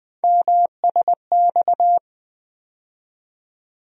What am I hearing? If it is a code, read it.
Morse code, heard as a keyed tone, about 21 wpm, 700 Hz. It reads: MSX
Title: USA Call Sign Postfixes - 3 Letters 20wpm